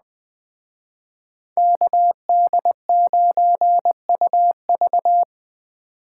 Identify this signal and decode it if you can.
Morse code, heard as a keyed tone, about 21 wpm, 700 Hz. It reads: KD9UV
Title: Speed Racing - USA 2x2 Call Signs 17wpm